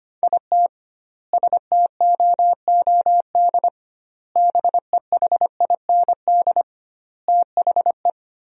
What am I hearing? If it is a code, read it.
Morse code, heard as a keyed tone, about 25 wpm, 700 Hz. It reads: IT STOOD BEHIND THE
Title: 2x Word-Spacing - Sentences from Top 500 Words - 25wpm